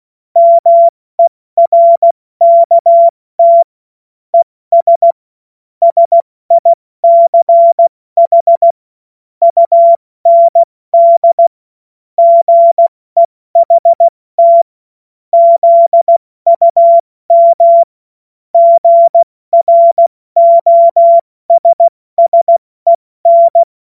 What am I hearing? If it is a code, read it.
Morse code, heard as a keyed tone, about 16 wpm, 680 Hz. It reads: MERKT ES SICH UND GEHT ZUM GROSSEN
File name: Morsekurs / Witze / Witz 06. Juli 2021, 70 F80